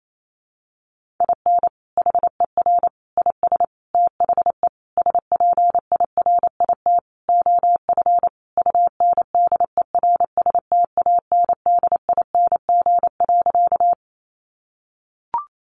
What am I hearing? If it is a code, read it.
Morse code, heard as a keyed tone, about 28 wpm, 700 Hz. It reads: IN HER IS THE SPIRIT OF UNDERSTANDING.
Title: Sentences from Top 1400 Words 28wpm